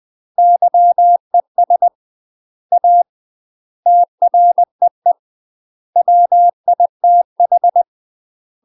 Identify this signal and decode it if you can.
Morse code, heard as a keyed tone, about 20 wpm, 700 Hz. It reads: YES A TREE WITH
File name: UK MASTER QSOs TRAINING/20WPM